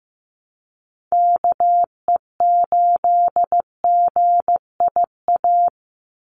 Morse code, heard as a keyed tone, about 15 wpm, 700 Hz. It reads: KE8GIA
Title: USA Call Signs 15wpm